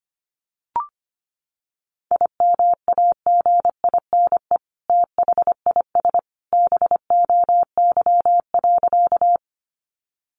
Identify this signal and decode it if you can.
Morse code, heard as a keyed tone, about 25 wpm, 700 Hz. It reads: IMAGINE THIS BOY.
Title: Sentences from Top 1400 Words 25wpm